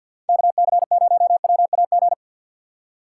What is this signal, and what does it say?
Morse code, heard as a keyed tone, about 52 wpm, 700 Hz. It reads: KC0WAG
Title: USA Call Signs - 6x Character Spacing - 50wpm